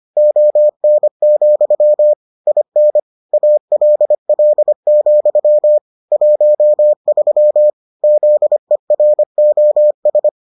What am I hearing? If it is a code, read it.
Morse code, heard as a keyed tone, about 25 wpm, 600 Hz. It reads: ON, IN ALL, 13 ZEROS